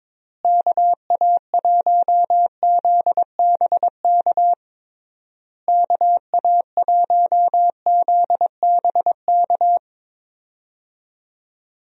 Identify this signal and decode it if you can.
Morse code, heard as a keyed tone, about 22 wpm, 700 Hz. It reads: KA1ZBK KA1ZBK
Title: USA Call Signs - 6 Characters - Repeated 3x - 22wpm